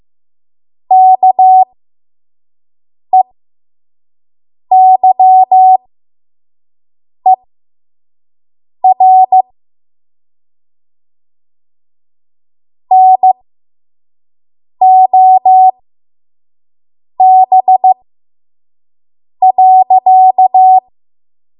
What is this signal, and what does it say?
Morse code, heard as a keyed tone, about 15 wpm, spaced as for 5 wpm, 750 Hz. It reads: KEYER NOB.
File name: Morse / 05WPM